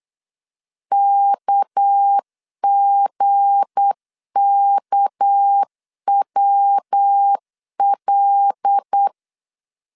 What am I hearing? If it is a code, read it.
Morse code, heard as a keyed tone, about 8 wpm, 800 Hz. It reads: KGKWL
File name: LEZIONI / 8 YJ45 / 8 Y 8wpm 3245